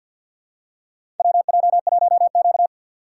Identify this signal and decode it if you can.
Morse code, heard as a keyed tone, about 51 wpm, 700 Hz. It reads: WJ1Q
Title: USA Call Signs - 4 Characters - 6x Character Spacing - 50wpm